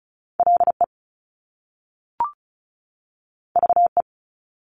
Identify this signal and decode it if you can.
Morse code, heard as a keyed tone, about 36 wpm, 700 Hz. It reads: LE VE
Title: Top Two Letter Combinations in Words 35wpm